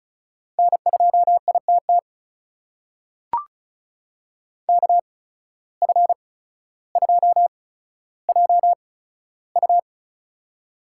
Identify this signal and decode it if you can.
Morse code, heard as a keyed tone, about 36 wpm, spaced as for 15 wpm, 700 Hz. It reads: N2ITT KF2JU
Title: USA Call Signs - 5 Characters - 8x Character Spacing - 35wpm